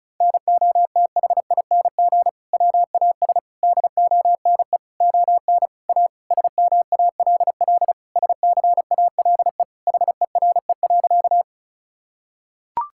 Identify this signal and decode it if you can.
Morse code, heard as a keyed tone, about 36 wpm, 700 Hz. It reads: NOTHING WAS DONE ON A SMALL SCALE HERE.
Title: Sentences from Top 1000 Words - Review Part 5 - 35wpm